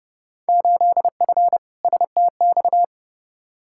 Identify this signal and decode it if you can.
Morse code, heard as a keyed tone, about 30 wpm, 700 Hz. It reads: 8F STX
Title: Field Day Exchanges Encore - 30wpm